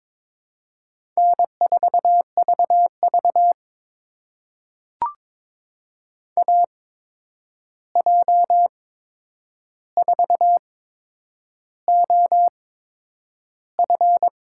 Morse code, heard as a keyed tone, about 22 wpm, spaced as for 9 wpm, 700 Hz. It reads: N4VV AJ4OF
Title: USA Call Signs - 8x Character Spacing - 22wpm